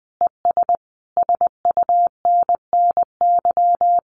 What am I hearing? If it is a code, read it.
Morse code, heard as a keyed tone, about 20 wpm, 700 Hz. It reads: ES SUNNY
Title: Common QSO Phrases Encore 20wpm